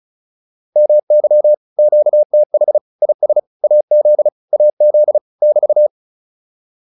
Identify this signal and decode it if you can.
Morse code, heard as a keyed tone, about 35 wpm, 600 Hz. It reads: MY QTH IS AZ AZ <BT>
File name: QSO209 35